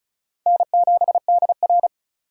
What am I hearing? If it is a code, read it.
Morse code, heard as a keyed tone, about 36 wpm, 700 Hz. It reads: N7DR
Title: K1USN SST QSO Practice - 35wpm